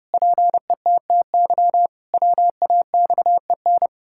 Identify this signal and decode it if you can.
Morse code, heard as a keyed tone, about 30 wpm, 700 Hz. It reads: PETTY WAXEN
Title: Pangram Challenge v2 - 30wpm